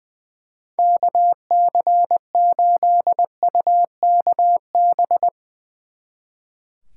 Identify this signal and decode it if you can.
Morse code, heard as a keyed tone, about 22 wpm, 700 Hz. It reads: KC8UKB
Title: Speed Racing - USA 2x3 Call Signs 20wpm